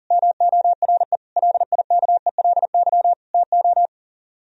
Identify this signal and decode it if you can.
Morse code, heard as a keyed tone, about 49 wpm, 700 Hz. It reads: MORE LIKELY TO
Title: Speed-Racing - Sets of 4 Words 40wpm